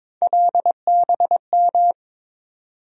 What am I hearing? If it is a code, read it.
Morse code, heard as a keyed tone, about 22 wpm, 700 Hz. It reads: LBM
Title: USA Call Sign Postfixes - 3 Letters 22wpm